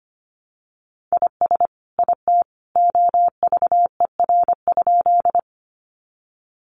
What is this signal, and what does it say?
Morse code, heard as a keyed tone, about 25 wpm, 700 Hz. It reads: IS IT OVER?
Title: Sentences from Top 100 Words 25wpm